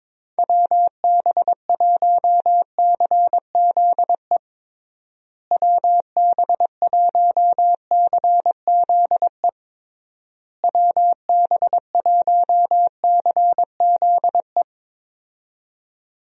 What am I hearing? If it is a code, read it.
Morse code, heard as a keyed tone, about 22 wpm, 700 Hz. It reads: WB1CZE WB1CZE WB1CZE
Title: USA Call Signs - 6 Characters - Repeated 3x - 22wpm